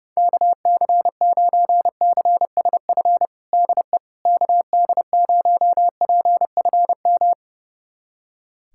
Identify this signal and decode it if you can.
Morse code, heard as a keyed tone, about 30 wpm, 700 Hz. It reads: KC9CSF DE KD0PFM
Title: USA Callsign DE Callsign - 6 Characters - 30wpm